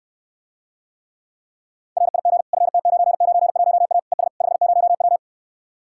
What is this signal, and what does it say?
Morse code, heard as a keyed tone, about 69 wpm, 700 Hz. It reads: RIG FT991A ES 50W